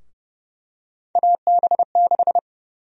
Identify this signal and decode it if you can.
Morse code, heard as a keyed tone, about 30 wpm, 700 Hz. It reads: AB6
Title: USA Call Sign Prefixes - 30wpm